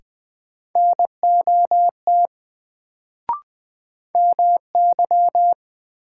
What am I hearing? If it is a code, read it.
Morse code, heard as a keyed tone, about 20 wpm, 700 Hz. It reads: NOT MY
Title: Sets of 2 Words - New Word Drill - 20wpm